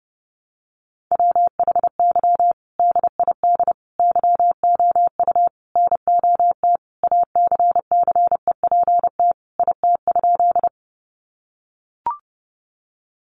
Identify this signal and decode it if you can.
Morse code, heard as a keyed tone, about 30 wpm, 700 Hz. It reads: WHY DID YOU NOT ACCEPT IT?
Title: Sentences from Top 1300 Words 30wpm